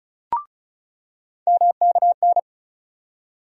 Morse code, heard as a keyed tone, about 36 wpm, 700 Hz. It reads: MKN
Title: USA Call Sign Postfixes - 3 Letters 35wpm